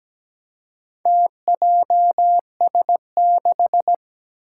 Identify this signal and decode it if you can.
Morse code, heard as a keyed tone, about 17 wpm, 700 Hz. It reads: TJS6
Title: Four Challenging Letter-Number Combinations 17wpm